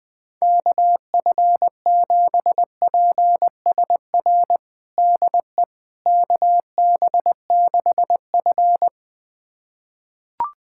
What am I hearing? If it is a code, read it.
Morse code, heard as a keyed tone, about 20 wpm, 700 Hz. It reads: KF7PSR DE KB6F
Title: USA Callsign DE Callsign 20wpm